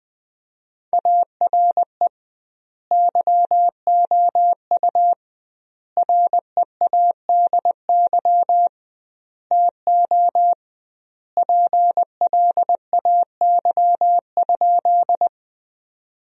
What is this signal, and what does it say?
Morse code, heard as a keyed tone, about 20 wpm, 700 Hz. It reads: ARE YOU READY TO PLAY?
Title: 2x Word-Spacing - Sentences from Top 400 Words - 20wpm